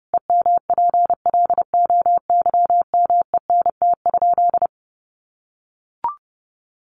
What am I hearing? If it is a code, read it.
Morse code, heard as a keyed tone, about 30 wpm, 700 Hz. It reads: EMPLOYMENT?
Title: Sentences from Top 1500 Words 30wpm